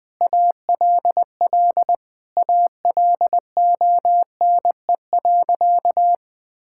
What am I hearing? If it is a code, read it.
Morse code, heard as a keyed tone, about 20 wpm, 700 Hz. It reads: ALL ALONE.
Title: Sentences from Top 1000 Words - Review Part 5 - 20wpm